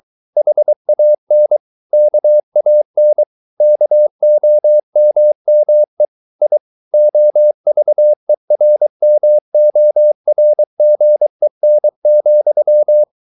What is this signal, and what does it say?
Morse code, heard as a keyed tone, about 23 wpm, 600 Hz. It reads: HAN KAN KOMME I OVERMORGEN,
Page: Ferie 23 wpm